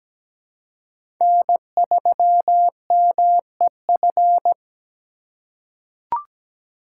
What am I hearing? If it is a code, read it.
Morse code, heard as a keyed tone, about 17 wpm, 700 Hz. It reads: N3MEF
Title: USA Call Signs - 5 Characters - 2x Character Spacing - 17wpm